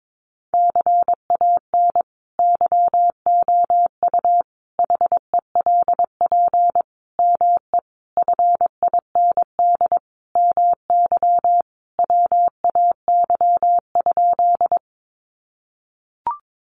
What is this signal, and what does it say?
Morse code, heard as a keyed tone, about 22 wpm, 700 Hz. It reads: CAN YOU HELP ME FIND MY WAY?
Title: Sentences from Top 200 Words 22wpm